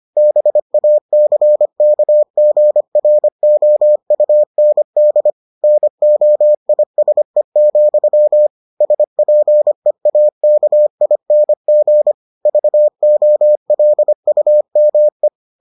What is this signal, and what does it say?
Morse code, heard as a keyed tone, about 25 wpm, 600 Hz. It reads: BACKGROUND NOISE, SPEAKING VOLUME